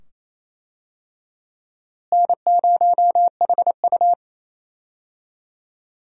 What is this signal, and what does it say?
Morse code, heard as a keyed tone, about 28 wpm, 700 Hz. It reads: N0HU
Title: USA Call Signs - 4 Characters - Repeated 2x - 28wpm